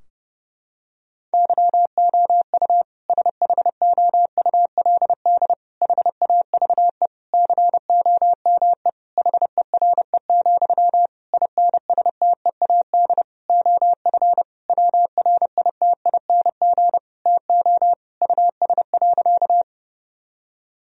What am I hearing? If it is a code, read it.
Morse code, heard as a keyed tone, about 30 wpm, 700 Hz. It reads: YOU SHOULD HAVE COME HERE, INSTEAD OF WRITING TO US.
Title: Sentences from Top 1000 Words - Review Part 4 - 30wpm